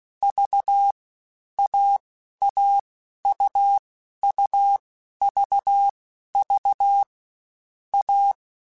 Morse code, heard as a keyed tone, about 16 wpm, 800 Hz. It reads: V A A U U V V A
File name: Difficult Combos / Difficult Combos-mp3s / AUV AUV4 @15